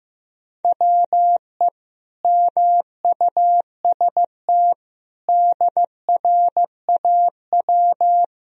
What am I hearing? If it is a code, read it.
Morse code, heard as a keyed tone, about 15 wpm, 700 Hz. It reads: WE MUST DRAW
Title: Sentences from Top 1000 Words - Review Part 1 - 15wpm